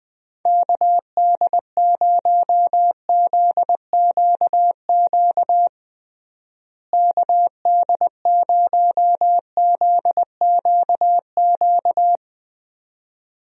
Morse code, heard as a keyed tone, about 20 wpm, 700 Hz. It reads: KD0ZQQ KD0ZQQ
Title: USA Call Signs - 6 Characters - Repeated 2x - 20wpm